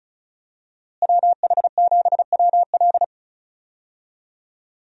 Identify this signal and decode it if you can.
Morse code, heard as a keyed tone, about 34 wpm, 700 Hz. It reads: WH7WL
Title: USA Call Signs - 5 Characters - Repeated 3x - 35wpm